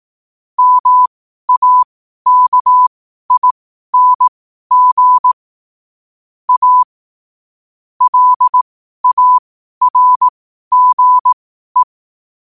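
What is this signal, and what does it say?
Morse code, heard as a keyed tone, about 18 wpm, spaced as for 12 wpm, 1000 Hz. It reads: MAKING A LARGE
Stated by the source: Morse code sent at 12 words per minute.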